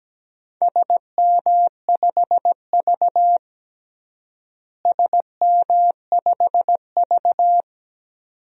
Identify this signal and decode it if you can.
Morse code, heard as a keyed tone, about 17 wpm, 700 Hz. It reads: SM5V SM5V
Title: DXCC-100 Call Signs - 4 Characters - Repeated 2x - 17wpm